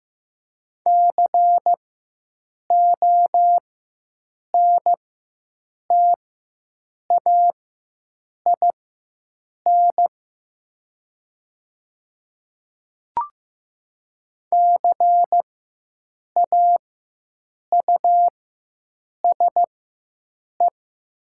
Morse code, heard as a keyed tone, about 15 wpm, spaced as for 7 wpm, 700 Hz. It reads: CONTAIN CAUSE
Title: 4x Character Spacing - Top 500 Words - 15wpm